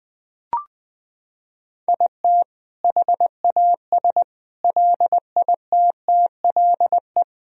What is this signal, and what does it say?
Morse code, heard as a keyed tone, about 20 wpm, 700 Hz. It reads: IT HAS LITTLE
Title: Sentences from Top 1000 Words - Review Part 4 - 20wpm